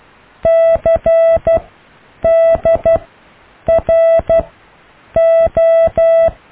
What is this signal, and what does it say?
Morse code, heard as a keyed tone, about 12 wpm, spaced as for 8 wpm, 660 Hz. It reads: CDRO
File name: cw / IJLUM+CDRO+EKTPV+BFHNW